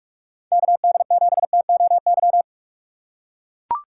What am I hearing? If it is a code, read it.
Morse code, heard as a keyed tone, about 44 wpm, 700 Hz. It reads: KD7TOY
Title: USA Call Signs - 6 Characters - 6x Character Spacing - 45wpm